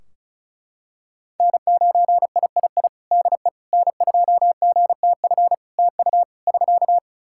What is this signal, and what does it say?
Morse code, heard as a keyed tone, about 36 wpm, 700 Hz. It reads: N9III DE N2GTF TU <SK>
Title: Practice QSO - Exchange 4 - Ending - 35wpm